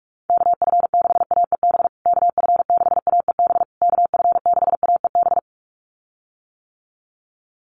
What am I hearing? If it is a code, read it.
Morse code, heard as a keyed tone, about 44 wpm, 700 Hz. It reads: KF6AEB KF6AEB KF6AEB
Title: USA Call Signs Repeated 3x 45wpm